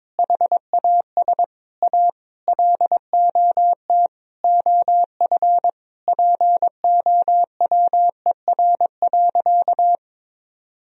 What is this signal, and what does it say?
Morse code, heard as a keyed tone, about 22 wpm, 700 Hz. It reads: HAS A LOT OF POWER.
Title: Sentences from Top 1000 Words - Review Part 4 - 22wpm